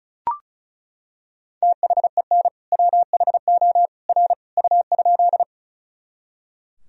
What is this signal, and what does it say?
Morse code, heard as a keyed tone, about 34 wpm, 700 Hz. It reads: THEN WHO R U?
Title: Sentences from Top 100 Words and Abbreviations 35wpm